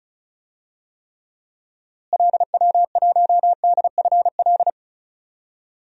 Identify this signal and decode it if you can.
Morse code, heard as a keyed tone, about 36 wpm, 700 Hz. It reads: LW1DFL
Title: DXCC-100 Call Signs - 6 Characters - Repeated 3x - 35wpm